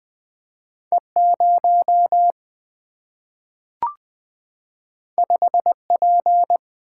Morse code, heard as a keyed tone, about 20 wpm, 700 Hz. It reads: E 0 5 P
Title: Single-Letter Single-Number 20wpm